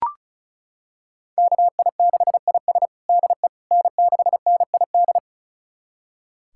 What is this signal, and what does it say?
Morse code, heard as a keyed tone, about 34 wpm, 690 Hz. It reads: KI6IS DE N6NID
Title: USA Callsign DE Callsign - 5 Characters - 35wpm